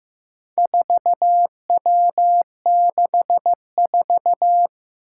Morse code, heard as a keyed tone, about 15 wpm, 700 Hz. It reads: 4W64
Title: Four Challenging Letter-Number Combinations 15wpm